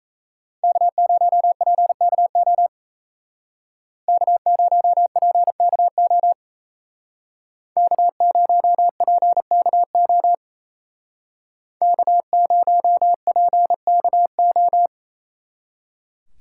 Morse code, heard as a keyed tone, about 33 wpm, 700 Hz. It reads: K0PKO K0PKO K0PKO K0PKO
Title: Speed Racing - USA Call Signs 28wpm